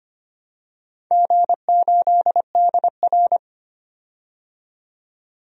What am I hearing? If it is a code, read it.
Morse code, heard as a keyed tone, about 25 wpm, 700 Hz. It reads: G8DR
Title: DXCC-100 Call Signs - 4 Characters - 25wpm